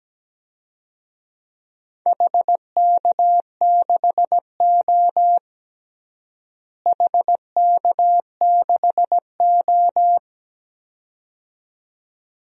Morse code, heard as a keyed tone, about 17 wpm, 700 Hz. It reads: HK6O HK6O
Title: DXCC-100 Call Signs - 4 Characters - Repeated 2x - 17wpm